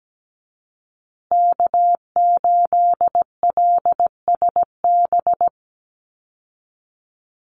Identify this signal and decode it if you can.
Morse code, heard as a keyed tone, about 17 wpm, 700 Hz. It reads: K8LSB
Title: Call Signs Repeated 3x - 17wpm